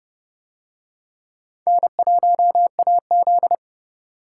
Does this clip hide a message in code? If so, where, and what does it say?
Yes: Morse code from 1.7 to 3.5 s, heard as a keyed tone, about 26 wpm, 700 Hz. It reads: N1AZ